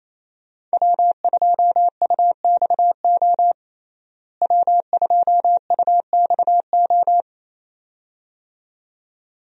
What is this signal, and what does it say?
Morse code, heard as a keyed tone, about 28 wpm, 700 Hz. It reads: W2UXO W2UXO
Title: USA Call Signs - 5 Characters - Repeated 2x - 28wpm